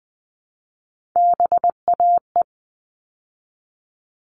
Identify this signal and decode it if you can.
Morse code, heard as a keyed tone, about 20 wpm, 700 Hz. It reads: BAE
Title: All Cut Numbers - 3-Digit 20wpm